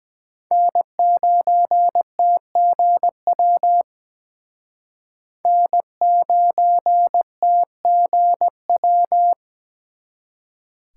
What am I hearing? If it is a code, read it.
Morse code, heard as a keyed tone, about 20 wpm, 700 Hz. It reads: N9TGW N9TGW
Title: Speed Racing - USA 1x3 Call Signs 17wpm